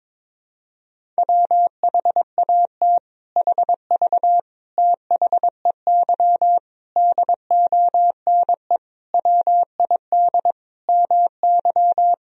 Morse code, heard as a keyed tone, about 22 wpm, 700 Hz. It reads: WHAT HV THEY DONE WID MY
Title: Sentences from Top 500 Words and Abbreviations 22wpm